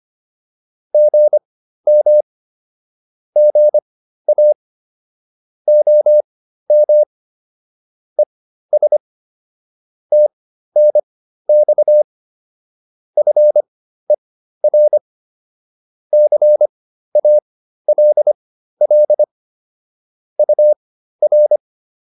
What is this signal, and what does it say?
Morse code, heard as a keyed tone, about 25 wpm, spaced as for 13 wpm, 600 Hz. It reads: GM GA OM ES TNX FER CALL UR
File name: QSO 407 13